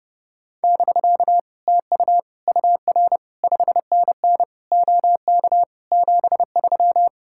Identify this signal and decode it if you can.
Morse code, heard as a keyed tone, about 30 wpm, 700 Hz. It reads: <BK> TU UR 5NN OK 73